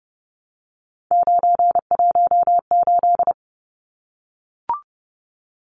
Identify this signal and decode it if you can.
Morse code, heard as a keyed tone, about 30 wpm, 700 Hz. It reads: 918
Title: Numbers - 3-Digit 30wpm